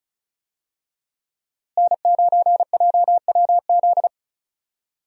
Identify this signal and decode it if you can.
Morse code, heard as a keyed tone, about 36 wpm, 700 Hz. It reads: N9JWZ
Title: USA Call Signs - 5 Characters - Repeated 2x - 35wpm